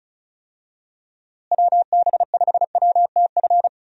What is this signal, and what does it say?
Morse code, heard as a keyed tone, about 36 wpm, 700 Hz. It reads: WB5WTF
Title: USA Call Signs - 6 Characters - Repeated 2x - 35wpm